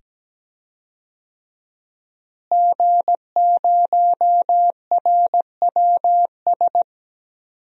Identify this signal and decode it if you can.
Morse code, heard as a keyed tone, about 17 wpm, 700 Hz. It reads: G0RWS
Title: DXCC-100 Call Signs - 5 Characters - Repeated 2x - 17wpm